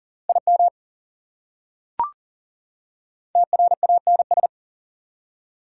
Morse code, heard as a keyed tone, about 40 wpm, 700 Hz. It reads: IM TRANS
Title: Prefixes and Suffixes 40wpm